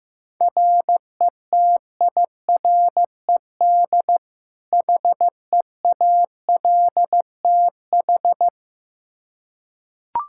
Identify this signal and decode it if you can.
Morse code, heard as a keyed tone, about 15 wpm, 700 Hz. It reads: RETIRED HEALTH
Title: Practice QSO - Exchange 3 - More Info - 15wpm